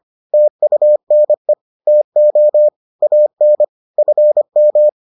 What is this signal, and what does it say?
Morse code, heard as a keyed tone, about 25 wpm, 600 Hz. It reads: TUNE TO AN FM